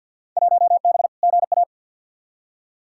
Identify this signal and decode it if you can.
Morse code, heard as a keyed tone, about 48 wpm, 700 Hz. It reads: 1B GA
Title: Field Day Exchanges Encore - 50wpm